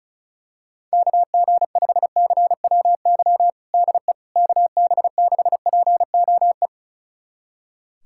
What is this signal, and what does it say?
Morse code, heard as a keyed tone, about 36 wpm, 700 Hz. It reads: KG5CWY DE KB6POE
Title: USA Callsign DE Callsign - 6 Characters - 35wpm